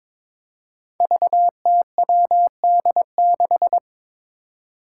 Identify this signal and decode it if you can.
Morse code, heard as a keyed tone, about 22 wpm, 700 Hz. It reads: VTWD6
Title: Five Challenging Letter-Number Combinations 22wpm